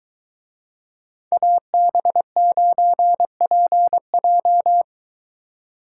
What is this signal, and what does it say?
Morse code, heard as a keyed tone, about 24 wpm, 700 Hz. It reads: AB9PJ